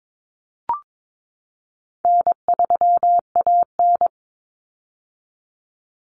Morse code, heard as a keyed tone, about 22 wpm, 700 Hz. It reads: N3AN
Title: Common Cut Numbers - 4-Digit 22wpm